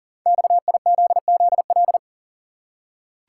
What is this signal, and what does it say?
Morse code, heard as a keyed tone, about 43 wpm, 700 Hz. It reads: XIZZL
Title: Code Groups - 5 Letters - 40wpm